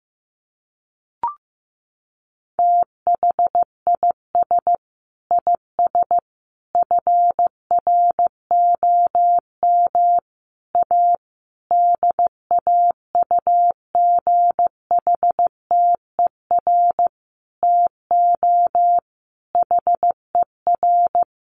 Morse code, heard as a keyed tone, about 15 wpm, 700 Hz. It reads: THIS IS FROM A DAUGHTER TO HER
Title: Sentences from Top 1500 Words 15wpm